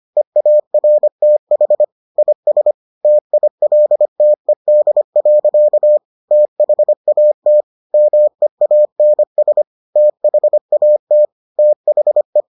Morse code, heard as a keyed tone, about 25 wpm, 600 Hz. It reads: EARTH IS TILTED. THAT MEANS THAT THE